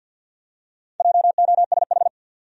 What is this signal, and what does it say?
Morse code, heard as a keyed tone, about 53 wpm, 700 Hz. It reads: JOSH
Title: USA Common Names - Max 5 Letters - 50wpm